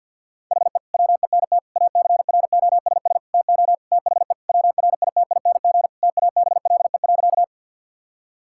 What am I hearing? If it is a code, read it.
Morse code, heard as a keyed tone, about 50 wpm, 700 Hz. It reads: HE WENT ACROSS TO THE WRITING TABLE.
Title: Sentences from Top 1000 Words - Review Part 5 - 50wpm